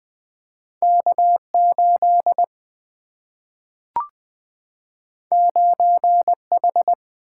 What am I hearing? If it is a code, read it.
Morse code, heard as a keyed tone, about 20 wpm, 700 Hz. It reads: K8 9H
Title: Single-Letter Single-Number 20wpm